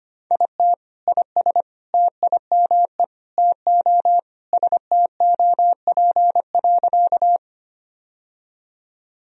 Morse code, heard as a keyed tone, about 25 wpm, 700 Hz. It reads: IT IS TIME TO STOP.
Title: Sentences from Top 1000 Words - Review Part 1 - 25wpm